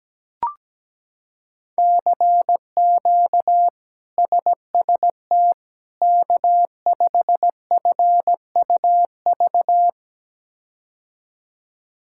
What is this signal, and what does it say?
Morse code, heard as a keyed tone, about 17 wpm, 700 Hz. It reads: CQ SST K5FUV
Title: K1USN SST QSO Practice - 17wpm